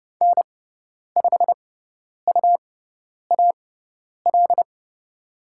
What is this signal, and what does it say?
Morse code, heard as a keyed tone, about 30 wpm, spaced as for 10 wpm, 700 Hz. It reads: N5UAL
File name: Lesson 17g - Letter u - Call Signs 30wpm